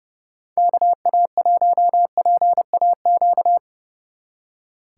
Morse code, heard as a keyed tone, about 38 wpm, 700 Hz. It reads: KA1PAQ